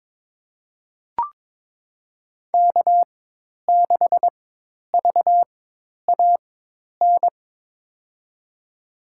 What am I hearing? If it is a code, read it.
Morse code, heard as a keyed tone, about 22 wpm, spaced as for 10 wpm, 700 Hz. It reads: K6VAN
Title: USA Call Signs - 5 Characters - 4x Character Spacing - 22wpm